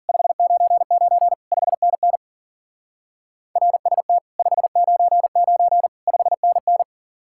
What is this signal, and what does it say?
Morse code, heard as a keyed tone, about 50 wpm, 700 Hz. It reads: 599 5NN RST 599 5NN
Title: Speed-Racing Common QSO Phrases Encore - 40wpm